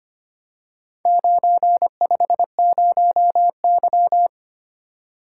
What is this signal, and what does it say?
Morse code, heard as a keyed tone, about 25 wpm, 690 Hz. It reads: 950Y
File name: practice / 25wpm / Four Letter-Number Combinations - 25wpm